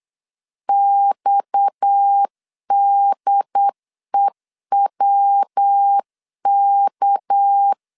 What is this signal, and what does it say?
Morse code, heard as a keyed tone, about 8 wpm, 800 Hz. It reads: XDEWK
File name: LEZIONI / 7 GF9Z / 7 GF 8wpm 3108